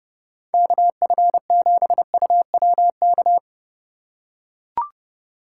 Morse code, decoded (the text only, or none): KF7UWK